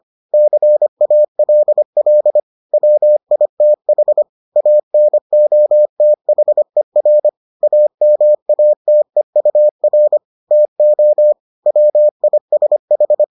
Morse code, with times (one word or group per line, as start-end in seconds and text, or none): 0.3-2.4 CALL
2.7-4.2 WITH
4.6-7.3 ANOTHER
7.6-10.2 AMATEUR
10.5-11.3 TO
11.7-13.2 WISH